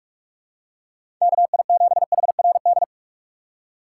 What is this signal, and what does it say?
Morse code, heard as a keyed tone, about 46 wpm, 700 Hz. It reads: KI7HRD
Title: USA Call Signs - 6x Character Spacing - 45wpm